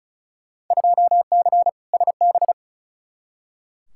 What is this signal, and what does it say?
Morse code, heard as a keyed tone, about 36 wpm, 700 Hz. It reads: 2C SB
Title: Field Day Exchanges Encore - 35wpm